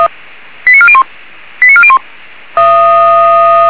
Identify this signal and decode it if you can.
Five-tone selective call (ZVEI), heard as quick runs of five tones, tones of about 70 ms. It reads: 89401 89401